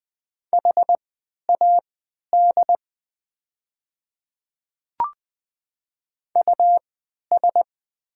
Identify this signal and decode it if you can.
Morse code, heard as a keyed tone, about 20 wpm, spaced as for 11 wpm, 690 Hz. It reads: HAD US
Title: 3x Character Spacing - Top 200 Words - 20wpm